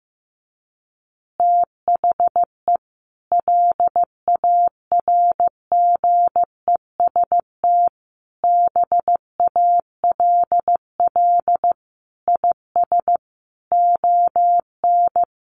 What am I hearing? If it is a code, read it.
Morse code, heard as a keyed tone, about 15 wpm, 700 Hz. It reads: THE LARGEST BALL IS ON
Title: Easy Phrases 15wpm